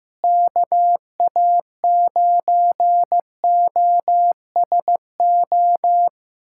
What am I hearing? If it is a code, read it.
Morse code, heard as a keyed tone, about 16 wpm, 700 Hz. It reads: KA9OSO
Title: USA Callsign DE Callsign - 6 Characters - 15wpm